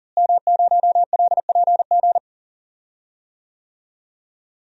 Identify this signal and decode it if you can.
Morse code, heard as a keyed tone, about 37 wpm, 700 Hz. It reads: M0LPG
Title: DXCC-100 Call Signs - 5 Characters - Repeated 2x - 40wpm